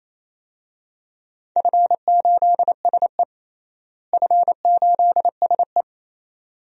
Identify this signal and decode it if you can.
Morse code, heard as a keyed tone, about 28 wpm, 700 Hz. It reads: F8SE F8SE
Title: DXCC-100 Call Signs - 4 Characters - Repeated 2x - 28wpm